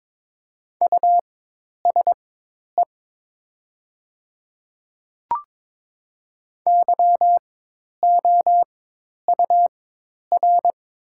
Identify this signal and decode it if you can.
Morse code, heard as a keyed tone, about 22 wpm, spaced as for 10 wpm, 700 Hz. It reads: USE YOUR
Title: 4x Character Spacing - Top 100 Words - 22wpm